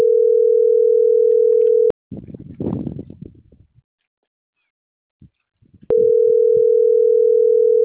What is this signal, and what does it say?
Ringback tone (North America), 440 + 480 Hz.